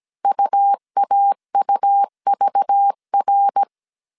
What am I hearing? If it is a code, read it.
Morse code, heard as a keyed tone, about 17 wpm, 800 Hz. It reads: UAUVR
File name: CW / DET / 1 9 1500C S A 16wpm 2828